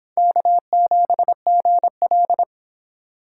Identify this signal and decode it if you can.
Morse code, heard as a keyed tone, about 25 wpm, 700 Hz. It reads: K7GL